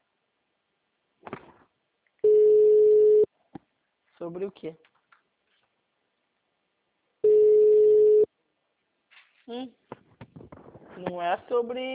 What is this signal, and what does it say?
Ringback tone (Europe, 425 Hz), with speech.